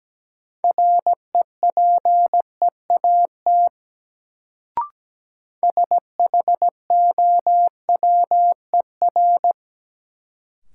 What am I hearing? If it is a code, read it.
Morse code, heard as a keyed tone, about 17 wpm, 700 Hz. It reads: REPEAT SHOWER
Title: Sentences from Top 1600 Words - New Word Drill - 17wpm